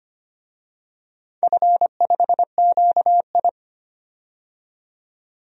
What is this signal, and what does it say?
Morse code, heard as a keyed tone, about 25 wpm, 700 Hz. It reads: F5QI
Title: DXCC-100 Call Signs - 4 Characters - 25wpm